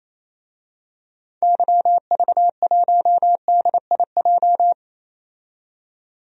Morse code, heard as a keyed tone, about 28 wpm, 700 Hz. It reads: YV1DIJ
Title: DXCC-100 Call Signs - 6 Characters - 28wpm